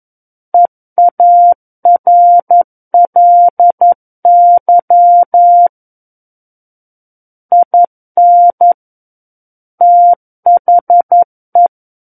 Morse code, heard as a keyed tone, about 11 wpm, 700 Hz. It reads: EARLY IN THE
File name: 11 WPM